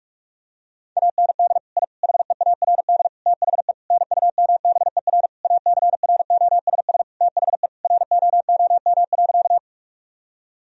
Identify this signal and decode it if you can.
Morse code, heard as a keyed tone, about 46 wpm, 700 Hz. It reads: AND I HEARD THE NUMBER ACROSS THE ROOM.
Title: Sentences from Top 1000 Words - Review Part 1 - 45wpm